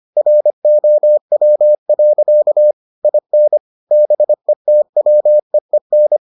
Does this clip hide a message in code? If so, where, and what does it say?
Yes: Morse code from 0.2 to 6.2 s, heard as a keyed tone, about 25 wpm, 600 Hz. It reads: ROW. IN BETWEEN